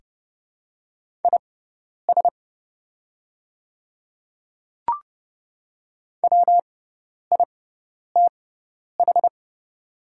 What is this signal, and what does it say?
Morse code, heard as a keyed tone, about 30 wpm, spaced as for 10 wpm, 700 Hz. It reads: IS WITH
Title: 6x Character Spacing - Top 100 Words - 30wpm